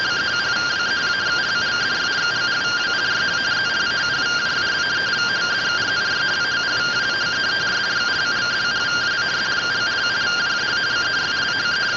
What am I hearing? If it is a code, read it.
Two-tone teleprinter signal, about 170 Hz shift.